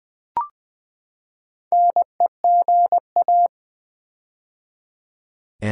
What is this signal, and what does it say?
Morse code, heard as a keyed tone, about 20 wpm, 700 Hz. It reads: NEGA
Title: Build A Word - 4 Letters At A Time - 8-Letter Words - 20wpm